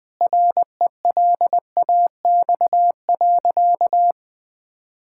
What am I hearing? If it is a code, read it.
Morse code, heard as a keyed tone, about 21 wpm, 700 Hz. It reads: RELAX.
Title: Sentences with 8-Word Phrases - Spoken Beginning - 20wpm